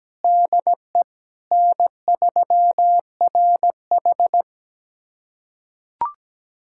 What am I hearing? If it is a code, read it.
Morse code, heard as a keyed tone, about 17 wpm, 700 Hz. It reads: DE N3RH
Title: USA Callsign DE Callsign - 4 Characters - 17wpm